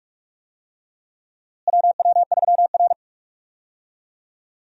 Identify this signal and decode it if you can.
Morse code, heard as a keyed tone, about 47 wpm, 700 Hz. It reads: WW3R
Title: USA Call Signs - 4 Characters - Repeated 2x - 45wpm